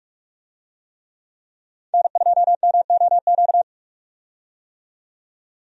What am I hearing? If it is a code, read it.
Morse code, heard as a keyed tone, about 46 wpm, 700 Hz. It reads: N2MOQ
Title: USA Call Signs - 5 Characters - 45wpm